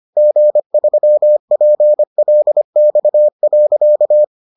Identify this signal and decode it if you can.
Morse code, heard as a keyed tone, about 25 wpm, 600 Hz. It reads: G3PLX.